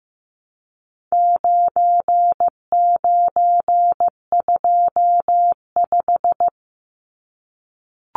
Morse code, heard as a keyed tone, about 15 wpm, 700 Hz. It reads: 9925
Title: Numbers - 4-Digit 15wpm